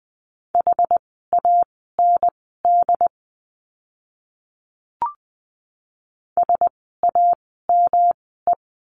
Morse code, heard as a keyed tone, about 20 wpm, spaced as for 14 wpm, 700 Hz. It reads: HAND SAME
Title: 115 Lesson 19f - Letter w - Words 20wpm